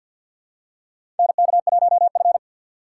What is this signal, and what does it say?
Morse code, heard as a keyed tone, about 51 wpm, 700 Hz. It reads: NK1F
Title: USA Call Signs - 3x Character Spacing - 50wpm